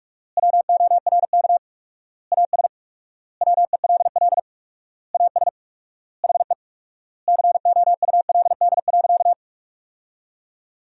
Morse code, heard as a keyed tone, about 46 wpm, 700 Hz. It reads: WORK AS WELL AS HE COULD.
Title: 4x Word-Spacing - Sentences from Top 500 Words - 45wpm